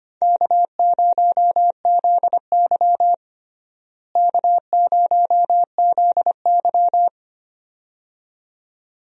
Morse code, heard as a keyed tone, about 25 wpm, 700 Hz. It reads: K0ZY K0ZY
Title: USA Call Signs - 4 Characters - Repeated 2x - 25wpm